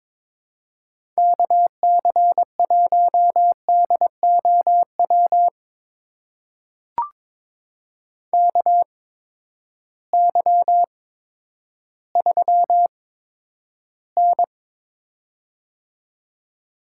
Morse code, heard as a keyed tone, about 22 wpm, spaced as for 11 wpm, 700 Hz. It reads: KC1DOW KY3N
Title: USA Call Signs - 8x Character Spacing - 22wpm